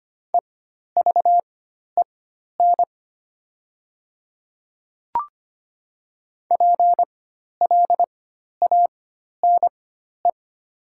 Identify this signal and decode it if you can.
Morse code, heard as a keyed tone, about 25 wpm, spaced as for 12 wpm, 690 Hz. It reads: EVEN PLANE
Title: 4x Character Spacing - Top 500 Words - 25wpm